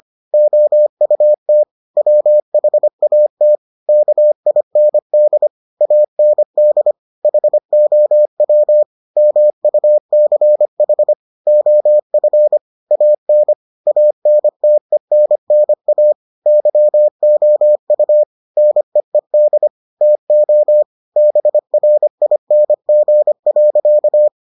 Morse code, heard as a keyed tone, about 25 wpm, 600 Hz. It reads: OUT WHAT KIND AND HOW MUCH OF AN ANTENNA YOU NEED TO BRING.